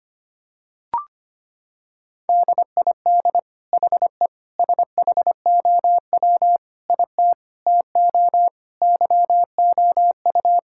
Morse code, heard as a keyed tone, about 25 wpm, 700 Hz. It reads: DID HE SHOW IT TO YOU
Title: Sentences from Top 1600 Words 25wpm